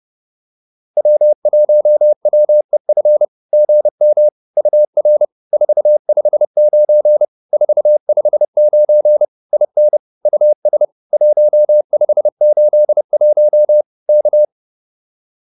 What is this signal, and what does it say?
Morse code, heard as a keyed tone, about 30 wpm, 590 Hz. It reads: W1WEF GM UR 459 459 IN US 1581 K